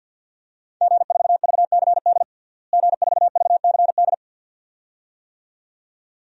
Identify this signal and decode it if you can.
Morse code, heard as a keyed tone, about 46 wpm, 700 Hz. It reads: G4VCD G4VCD
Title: DXCC-100 Call Signs - 5 Characters - Repeated 2x - 50wpm